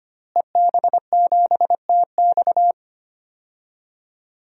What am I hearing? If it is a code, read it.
Morse code, heard as a keyed tone, about 25 wpm, 700 Hz. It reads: EB7TX
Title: DXCC-100 Call Signs - 5 Characters - Repeated 2x - 25wpm